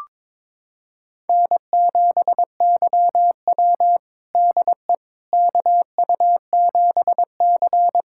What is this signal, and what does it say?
Morse code, heard as a keyed tone, about 22 wpm, 700 Hz. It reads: N7YW DE KU7C
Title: USA Callsign DE Callsign - 4 Characters - 22wpm